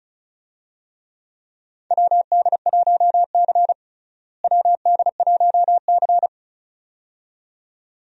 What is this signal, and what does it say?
Morse code, heard as a keyed tone, about 34 wpm, 700 Hz. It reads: WD1C WD1C
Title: USA Call Signs - 4 Characters - Repeated 2x - 35wpm